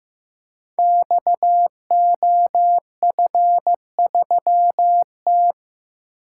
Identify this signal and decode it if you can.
Morse code, heard as a keyed tone, about 15 wpm, 700 Hz. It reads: XOF3T
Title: Code Groups - 5 Letter:Numbers - 15wpm